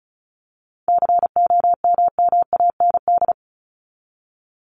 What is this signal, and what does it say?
Morse code, heard as a keyed tone, about 34 wpm, 700 Hz. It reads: COMMAND
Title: Top 1501-1750 Words 35wpm